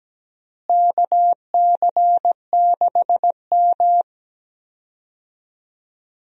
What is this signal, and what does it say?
Morse code, heard as a keyed tone, about 17 wpm, 700 Hz. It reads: KC6M
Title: USA Call Signs - 4 Characters - Repeated 3x - 17wpm